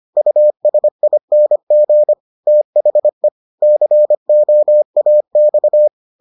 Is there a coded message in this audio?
Yes: Morse code from 0.2 to 5.9 s, heard as a keyed tone, about 25 wpm, 600 Hz. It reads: USING THE COAX